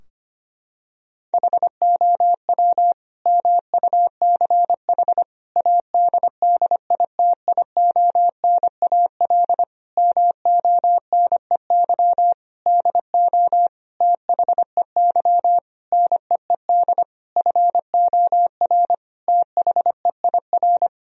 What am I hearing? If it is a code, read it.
Morse code, heard as a keyed tone, about 25 wpm, 700 Hz. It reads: HOW MUCH ADDITIONAL MONEY DO THEY NEED FOR THEIR
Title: Sentences from Top 1000 Words - Review Part 4 - 25wpm